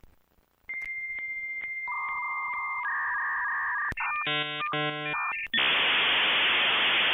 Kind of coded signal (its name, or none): dial-up modem handshake